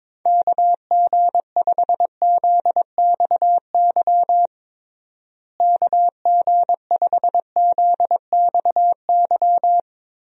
Morse code, heard as a keyed tone, about 22 wpm, 700 Hz. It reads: KG5ZXY KG5ZXY
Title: USA Call Signs - 6 Characters - Repeated 2x - 22wpm